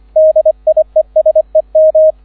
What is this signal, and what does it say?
Morse code, heard as a keyed tone, about 25 wpm, 630 Hz, with radio noise. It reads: DIESEM